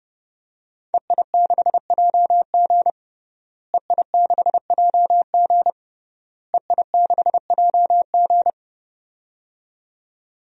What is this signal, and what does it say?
Morse code, heard as a keyed tone, about 30 wpm, 700 Hz. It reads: EI6JG EI6JG EI6JG
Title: DXCC-100 Call Signs - 5 Characters - Repeated 3x - 30wpm